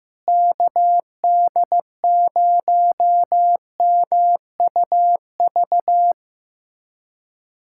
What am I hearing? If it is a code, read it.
Morse code, heard as a keyed tone, about 15 wpm, 700 Hz. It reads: KD0MUV
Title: USA Callsign DE Callsign 15wpm